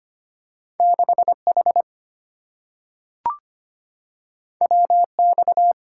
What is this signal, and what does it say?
Morse code, heard as a keyed tone, about 25 wpm, 700 Hz. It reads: 6H WX
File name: Two Letter-Number Combinations - 25wpm